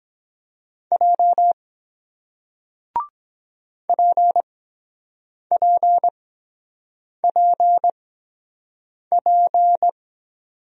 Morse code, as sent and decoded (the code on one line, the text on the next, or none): .--- / .--. .--. .--. .--.
J PPPP